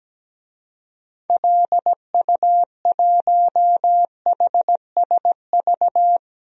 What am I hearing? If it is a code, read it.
Morse code, heard as a keyed tone, about 17 wpm, 700 Hz. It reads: LU1HSV
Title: DXCC-100 Call Signs - 6 Characters - 17wpm